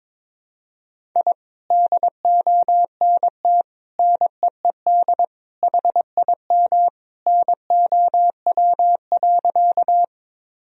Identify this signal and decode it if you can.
Morse code, heard as a keyed tone, about 22 wpm, 700 Hz. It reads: I DONT NEED HIM NOW.
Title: Sentences from Top 1000 Words - Review Part 1 - 22wpm